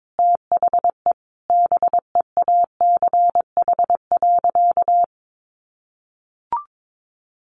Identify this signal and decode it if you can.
Morse code, heard as a keyed tone, about 22 wpm, 700 Hz. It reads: THE BEACH.
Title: Sentences from Top 1500 Words 22wpm